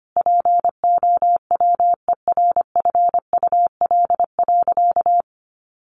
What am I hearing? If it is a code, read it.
Morse code, heard as a keyed tone, about 25 wpm, 700 Hz. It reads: POWERFUL.
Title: Sentences from Top 1300 Words 25wpm